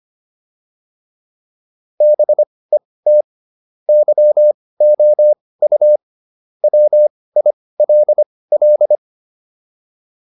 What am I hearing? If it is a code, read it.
Morse code, heard as a keyed tone, about 25 wpm, spaced as for 18 wpm, 600 Hz. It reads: BET YOU WILL
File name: PR305 18